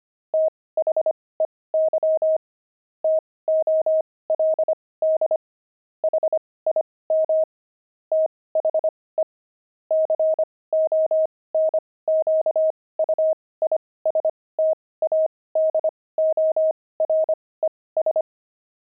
Morse code, heard as a keyed tone, about 25 wpm, spaced as for 18 wpm, 630 Hz. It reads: THEY TOLD HIM THE CONQUISTADORES
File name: pangram-fs-18-cs-25-kidding